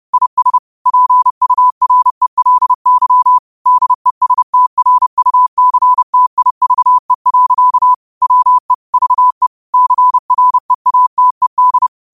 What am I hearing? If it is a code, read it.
Morse code, heard as a keyed tone, about 30 wpm, 1000 Hz. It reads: IS PURELY DESTRUCTIVE. WE VE CREATED
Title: 30 WPM morse code quotes for Fri, 01 Mar 2024